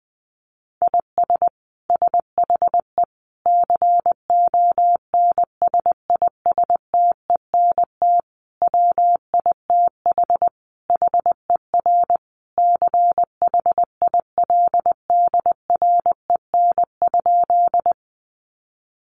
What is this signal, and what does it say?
Morse code, heard as a keyed tone, about 20 wpm, 700 Hz. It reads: IS SHE CONSISTENT WITH HER CHILDREN?
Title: Sentences from Top 1500 Words 20wpm